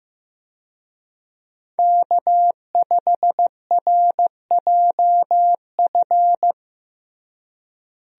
Morse code, heard as a keyed tone, about 15 wpm, 700 Hz. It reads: K5RJF
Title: USA Call Signs - 5 Characters - Repeated 2x - 15wpm